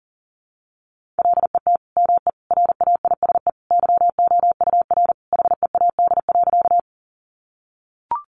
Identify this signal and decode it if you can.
Morse code, heard as a keyed tone, about 40 wpm, 700 Hz. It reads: LET ME RAISE YOUR HEAD.
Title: Sentences from Top 1400 Words 40wpm